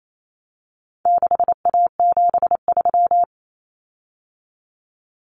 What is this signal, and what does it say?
Morse code, heard as a keyed tone, about 28 wpm, 700 Hz. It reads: 6A73
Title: Common Cut Numbers - 4-Digit 28wpm